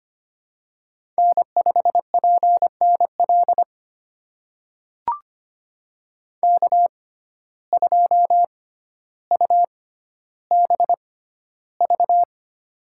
Morse code, heard as a keyed tone, about 25 wpm, spaced as for 13 wpm, 700 Hz. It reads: N5PNL K2UBV
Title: USA Call Signs - 5 Characters - 6x Character Spacing - 25wpm